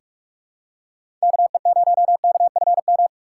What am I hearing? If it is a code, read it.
Morse code, heard as a keyed tone, about 46 wpm, 700 Hz. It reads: KE0KFM
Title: USA Call Signs - 3x Character Spacing - 45wpm